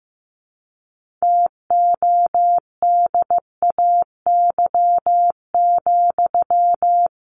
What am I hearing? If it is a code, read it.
Morse code, heard as a keyed tone, about 15 wpm, 700 Hz. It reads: TODAY,
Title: Sentences from Top 1400 Words 15wpm